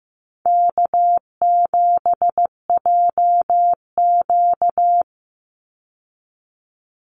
Morse code, heard as a keyed tone, about 15 wpm, 700 Hz. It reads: K7JQ
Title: Call Signs Repeated 3x - 15wpm